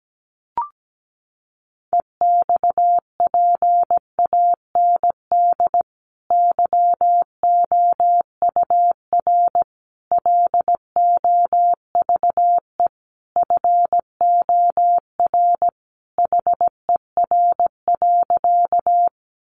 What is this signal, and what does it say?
Morse code, heard as a keyed tone, about 17 wpm, 700 Hz. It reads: EXPAND YOUR LOVE FOR HER.
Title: Sentences from Top 1400 Words 17wpm